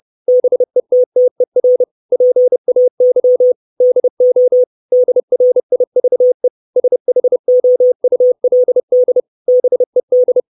Morse code, heard as a keyed tone, about 30 wpm, 500 Hz. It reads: BETTER PAY DO DRIVE SHOULD BED
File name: 632 Words 30WPM